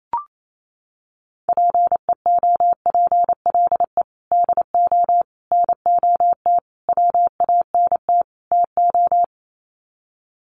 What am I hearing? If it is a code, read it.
Morse code, heard as a keyed tone, about 28 wpm, 700 Hz. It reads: PEOPLE DO NOT WANT TO
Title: Sets of 5 Words Encore2 28wpm